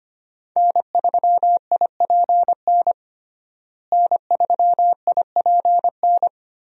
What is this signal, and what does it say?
Morse code, heard as a keyed tone, about 25 wpm, 700 Hz. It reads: N3IPN N3IPN
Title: USA Call Signs - 5 Characters - Repeated 3x - 25wpm